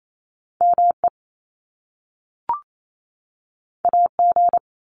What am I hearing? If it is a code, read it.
Morse code, heard as a keyed tone, about 28 wpm, 700 Hz. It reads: ME AG
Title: Top Two Letter Combinations in Words 28wpm